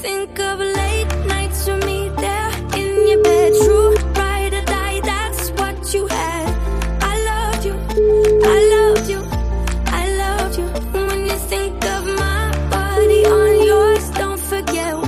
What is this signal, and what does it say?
Ringback tone (Europe, 425 Hz), with noise.